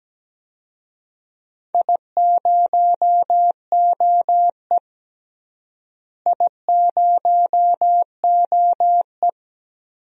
Morse code, heard as a keyed tone, about 17 wpm, 700 Hz. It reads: I0OE I0OE
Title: DXCC-100 Call Signs - 4 Characters - Repeated 2x - 17wpm